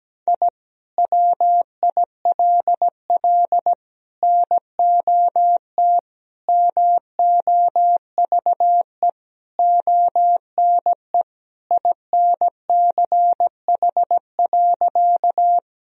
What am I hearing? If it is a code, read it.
Morse code, heard as a keyed tone, about 17 wpm, 700 Hz. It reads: I WILL NOT MOVE ONE INCH.
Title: Sentences from Top 1000 Words - Review Part 3 - 17wpm